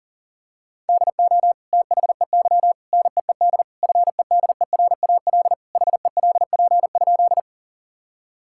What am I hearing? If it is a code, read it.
Morse code, heard as a keyed tone, about 40 wpm, 700 Hz. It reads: DO THEY NEED FEDERAL HELP?
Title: Sentences from Top 1000 Words - Review Part 5 - 40wpm